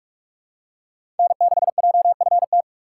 Morse code, heard as a keyed tone, about 46 wpm, 700 Hz. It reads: N6JFT
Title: USA Call Signs - 5 Characters - 2x Character Spacing - 45wpm